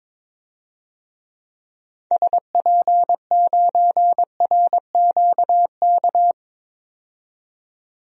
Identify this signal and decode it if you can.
Morse code, heard as a keyed tone, about 21 wpm, 700 Hz. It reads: SP9RQK
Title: DXCC-100 Call Signs - 6 Characters - 22wpm